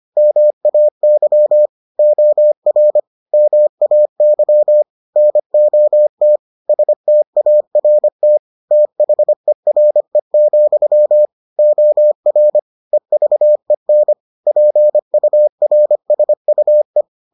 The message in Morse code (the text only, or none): MAY OR MAY NOT START THERE, OR EVEN PURSUE